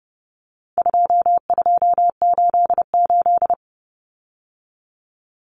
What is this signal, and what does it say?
Morse code, heard as a keyed tone, about 30 wpm, 700 Hz. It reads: 2288
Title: Common Cut Numbers - 4-Digit 30wpm